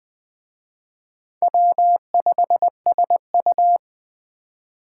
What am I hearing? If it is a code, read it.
Morse code, heard as a keyed tone, about 20 wpm, 700 Hz. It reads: W5SU